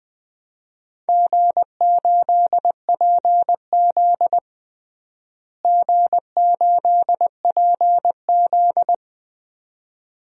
Morse code, heard as a keyed tone, about 20 wpm, 700 Hz. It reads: G8PZ G8PZ
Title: DXCC-100 Call Signs - 4 Characters - Repeated 2x - 20wpm